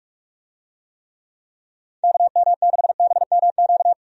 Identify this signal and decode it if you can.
Morse code, heard as a keyed tone, about 46 wpm, 700 Hz. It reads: KM6BMQ
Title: USA Call Signs - 6 Characters - 45wpm